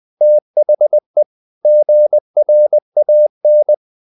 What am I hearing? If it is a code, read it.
Morse code, heard as a keyed tone, about 20 wpm, 600 Hz. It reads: THE GRAN
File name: the old man and the sea / the old man and the sea part 1 20wpm